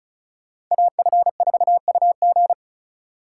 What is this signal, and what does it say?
Morse code, heard as a keyed tone, about 46 wpm, 700 Hz. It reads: AF4UG